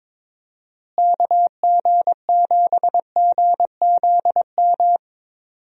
Morse code, heard as a keyed tone, about 22 wpm, 700 Hz. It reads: KG7GZM
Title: USA Call Signs - 6 Characters - Repeated 2x - 22wpm